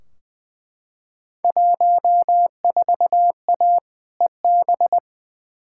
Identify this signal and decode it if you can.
Morse code, heard as a keyed tone, about 20 wpm, 700 Hz. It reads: 14A EB
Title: Field Day Exchanges Encore - 20wpm